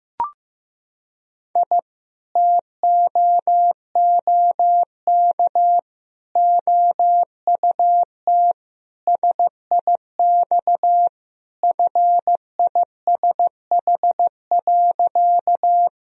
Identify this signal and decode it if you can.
Morse code, heard as a keyed tone, about 16 wpm, 700 Hz. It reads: I TOOK OUT SIX FISH.
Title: Sentences from Top 1000 Words - Review Part 2 - 15wpm